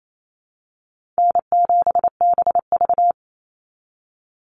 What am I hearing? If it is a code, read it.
Morse code, heard as a keyed tone, about 28 wpm, 700 Hz. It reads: N7BV
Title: Easy Call Signs Repeated 2x - 28wpm